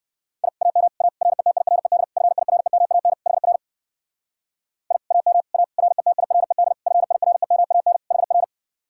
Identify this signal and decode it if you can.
Morse code, heard as a keyed tone, about 55 wpm, 700 Hz. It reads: I AM A RETIRED FIREMAN HR I AM A RETIRED FIREMAN HR
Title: Speed-Racing Common QSO Phrases Encore - 45wpm